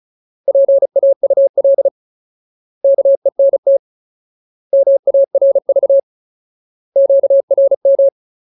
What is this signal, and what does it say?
Morse code, heard as a keyed tone, about 34 wpm, 550 Hz. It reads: PAUL KENT MARV QRM
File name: Ultimate Word List / 35WPM Ultimate